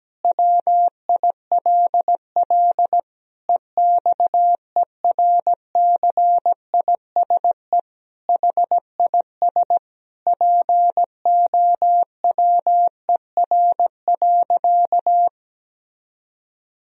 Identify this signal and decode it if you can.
Morse code, heard as a keyed tone, about 17 wpm, 700 Hz. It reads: WILL EXERCISE HIS POWER.
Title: Sentences from Top 1000 Words - Review Part 4 - 17wpm